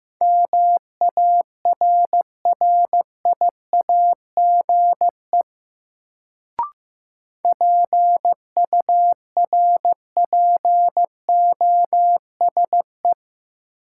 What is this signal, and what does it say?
Morse code, heard as a keyed tone, about 15 wpm, 700 Hz. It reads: MARRIAGE PURPOSE
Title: Sentences from Top 800 Words - New Word Drill - 15wpm